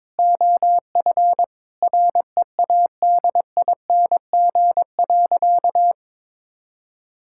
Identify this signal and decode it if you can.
Morse code, heard as a keyed tone, about 22 wpm, 700 Hz. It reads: OF READING.
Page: morse-code-ninja-practice-22wpm